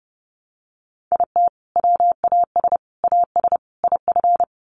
Morse code, heard as a keyed tone, about 30 wpm, 700 Hz. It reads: IT WAS AS IF
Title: Sets of 4 Words 30wpm